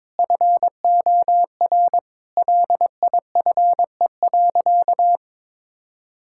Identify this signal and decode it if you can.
Morse code, heard as a keyed tone, about 22 wpm, 700 Hz. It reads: FOR LIFE.
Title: Sentences from Top 1000 Words - Review Part 6 - 22wpm